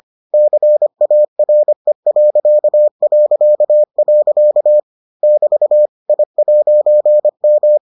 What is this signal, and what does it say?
Morse code, heard as a keyed tone, about 25 wpm, 600 Hz. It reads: CARE... <BT> I'M